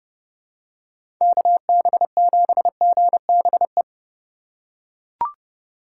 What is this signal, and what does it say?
Morse code, heard as a keyed tone, about 32 wpm, 700 Hz. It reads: KB7GBE
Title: USA Call Signs - 6 Characters - 3x Character Spacing - 30wpm